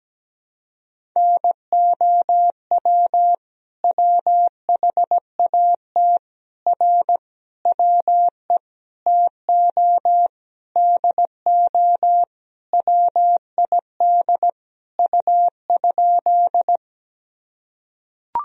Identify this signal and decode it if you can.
Morse code, heard as a keyed tone, about 17 wpm, 700 Hz. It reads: NOW WHAT R WE TO DO WID U?
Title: Sentences from Top 100 Words and Abbreviations 17wpm